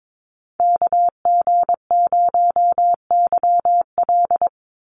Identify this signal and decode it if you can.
Morse code, heard as a keyed tone, about 22 wpm, 700 Hz. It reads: KG0YL
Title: Call Signs - 22wpm